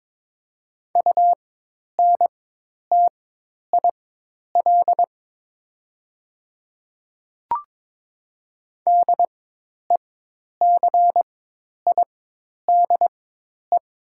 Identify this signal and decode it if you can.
Morse code, heard as a keyed tone, about 22 wpm, spaced as for 10 wpm, 700 Hz. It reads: UNTIL DECIDE
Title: 4x Character Spacing - Top 500 Words - 22wpm